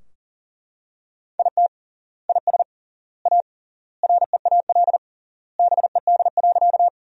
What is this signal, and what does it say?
Morse code, heard as a keyed tone, about 40 wpm, 700 Hz. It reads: IT IS A REAL BED.
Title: 3x Word-Spacing - Sentences from Top 500 Words - 40wpm